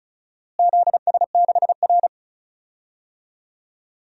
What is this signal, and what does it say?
Morse code, heard as a keyed tone, about 34 wpm, 700 Hz. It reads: ZS6R
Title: DXCC-100 Call Signs - 4 Characters - Repeated 2x - 35wpm